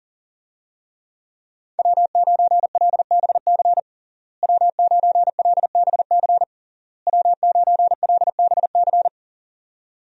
Morse code, heard as a keyed tone, about 40 wpm, 700 Hz. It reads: W9LBC W9LBC W9LBC
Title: USA Call Signs - 5 Characters - Repeated 3x - 40wpm